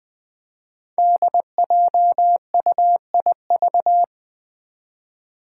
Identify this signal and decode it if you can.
Morse code, heard as a keyed tone, about 20 wpm, 700 Hz. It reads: DJUIV
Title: Five Challenging Letter-Number Combinations 20wpm